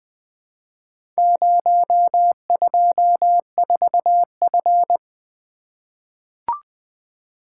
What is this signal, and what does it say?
Morse code, heard as a keyed tone, about 20 wpm, 700 Hz. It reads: 024F
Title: Four Letter-Number Combinations - 20wpm